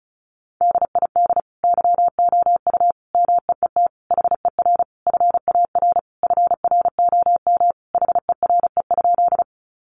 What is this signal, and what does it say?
Morse code, heard as a keyed tone, about 34 wpm, 700 Hz. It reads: DID YOU MEET HER FAR FROM HERE?
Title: Sentences from Top 1100 Words 35wpm